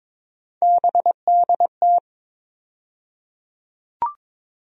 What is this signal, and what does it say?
Morse code, heard as a keyed tone, about 22 wpm, 700 Hz. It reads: BDT
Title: USA Call Sign Postfixes - 3 Letters 22wpm